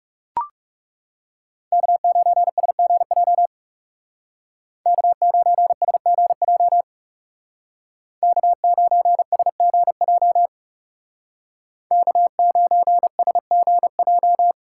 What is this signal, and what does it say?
Morse code, heard as a keyed tone, about 34 wpm, 700 Hz. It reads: K9SGJ K9SGJ K9SGJ K9SGJ
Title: Speed Racing - USA 1x3 Call Signs 30wpm